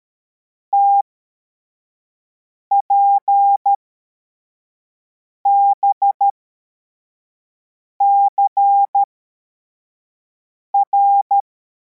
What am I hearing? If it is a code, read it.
Morse code, heard as a keyed tone, about 13 wpm, spaced as for 4 wpm, 800 Hz. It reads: TPBCR